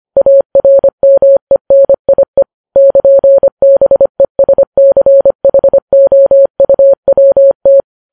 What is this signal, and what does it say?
Morse code, heard as a keyed tone, about 25 wpm, 550 Hz. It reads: ARMENIE <KN>BESCHOUWT
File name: Files Web tls / Morse on9fv / morse25kps